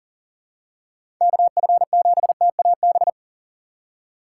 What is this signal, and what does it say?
Morse code, heard as a keyed tone, about 37 wpm, 700 Hz. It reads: KF7TAB
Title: USA Call Signs - 6 Characters - 3x Character Spacing - 40wpm